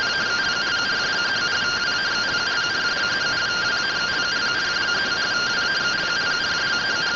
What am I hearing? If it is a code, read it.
Two-tone teleprinter signal, about 170 Hz shift.
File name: findings / 100 surview 0.gen-38 rtty rx